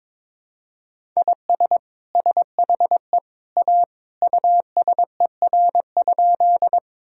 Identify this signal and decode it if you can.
Morse code, heard as a keyed tone, about 22 wpm, 700 Hz. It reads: IS SHE A USER?
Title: Sentences from Top 1000 Words - Review Part 5 - 22wpm